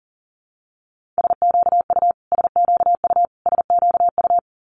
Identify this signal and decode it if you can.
Morse code, heard as a keyed tone, about 40 wpm, 690 Hz. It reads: SQU SQU SQU
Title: Multi-letter Phonemes Repeated 3x 40wpm